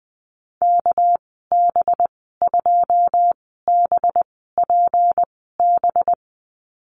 Morse code, heard as a keyed tone, about 20 wpm, spaced as for 14 wpm, 700 Hz. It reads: KB2BPB
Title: 200 Lesson 32g - Letter k - Call Signs 20wpm